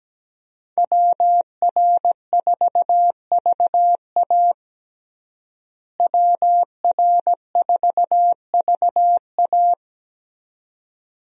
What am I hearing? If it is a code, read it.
Morse code, heard as a keyed tone, about 17 wpm, 700 Hz. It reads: WR4VA WR4VA
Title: USA Call Signs - 5 Characters - Repeated 2x - 17wpm